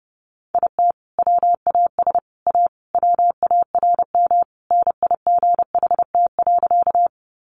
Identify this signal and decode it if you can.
Morse code, heard as a keyed tone, about 30 wpm, 700 Hz. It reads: IT WAS A WARM NIGHT.
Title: Sentences from Top 500 Words 30wpm